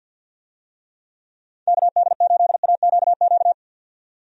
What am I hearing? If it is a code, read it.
Morse code, heard as a keyed tone, about 48 wpm, 700 Hz. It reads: KD8AQQ
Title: USA Call Signs - 6 Characters - 50wpm